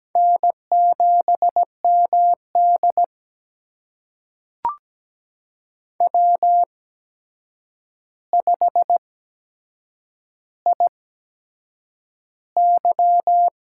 Morse code, heard as a keyed tone, about 17 wpm, spaced as for 7 wpm, 700 Hz. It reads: N7MD W5IY
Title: USA Call Signs - 4 Characters - 8x Character Spacing - 17wpm